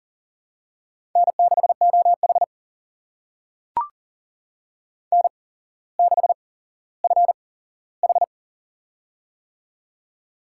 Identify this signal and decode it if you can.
Morse code, heard as a keyed tone, about 40 wpm, spaced as for 17 wpm, 700 Hz. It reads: N6OH N6FH
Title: USA Call Signs - 8x Character Spacing - 40wpm